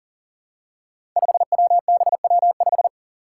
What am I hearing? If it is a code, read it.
Morse code, heard as a keyed tone, about 42 wpm, 700 Hz. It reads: 5WBW5
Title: Five Challenging Letter-Number Combinations 40wpm